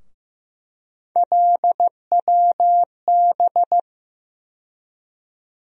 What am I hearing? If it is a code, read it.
Morse code, heard as a keyed tone, about 15 wpm, 700 Hz. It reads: LWB
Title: USA Call Sign Postfixes - 3 Letters 15wpm